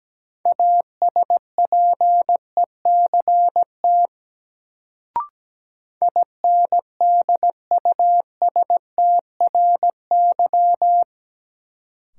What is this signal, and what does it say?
Morse code, heard as a keyed tone, about 17 wpm, 700 Hz. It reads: ASPECT INDUSTRY
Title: Sentences from Top 1300 Words - New Word Drill Alt - 17wpm